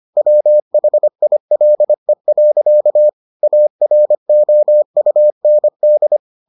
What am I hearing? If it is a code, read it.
Morse code, heard as a keyed tone, about 25 wpm, 600 Hz. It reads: WHILE. AROUND